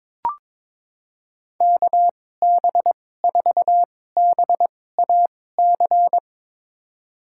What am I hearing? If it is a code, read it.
Morse code, heard as a keyed tone, about 21 wpm, spaced as for 16 wpm, 700 Hz. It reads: KB4BAC
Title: USA Call Signs - 6 Characters - 2x Character Spacing - 22wpm